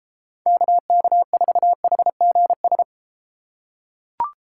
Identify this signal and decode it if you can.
Morse code, heard as a keyed tone, about 32 wpm, 700 Hz. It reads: KK4HGS